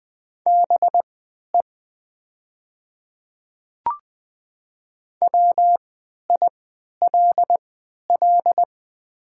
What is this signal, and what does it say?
Morse code, heard as a keyed tone, about 20 wpm, spaced as for 11 wpm, 700 Hz. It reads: BE WILL
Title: 3x Character Spacing - Top 100 Words - 20wpm